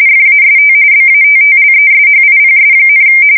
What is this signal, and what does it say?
RTTY, about 45.45 baud. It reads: FAST BATTLESHIPS,
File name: fsk170 / USS-New-Jersey